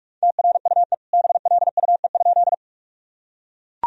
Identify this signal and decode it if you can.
Morse code, heard as a keyed tone, about 41 wpm, 700 Hz. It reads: TRUE BLUE?
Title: Sentences from Top 1000 Words - Review Part 3 - 45wpm